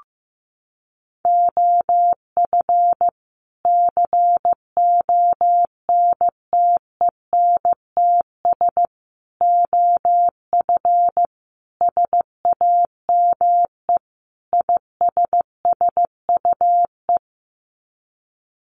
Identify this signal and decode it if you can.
Morse code, heard as a keyed tone, about 15 wpm, 700 Hz. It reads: OF CONTENTS OF SAME ISSUE
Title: Sets of 5 Words Encore2 15wpm